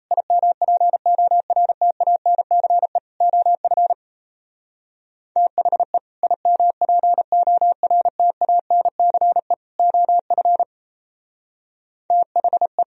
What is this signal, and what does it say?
Morse code, heard as a keyed tone, about 34 wpm, 700 Hz. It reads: IMPORTANCE OF THE IMPORTANCE OF THE
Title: Speed-Racing - Sets of 3 Words 28wpm